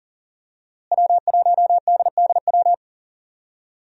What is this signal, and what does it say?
Morse code, heard as a keyed tone, about 40 wpm, 700 Hz. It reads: W1DDW
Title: USA Call Signs - 5 Characters - Repeated 2x - 40wpm